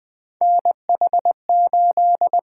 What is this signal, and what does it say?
Morse code, heard as a keyed tone, about 20 wpm, 700 Hz. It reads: NH8
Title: Three Letter-Number Combinations - 20wpm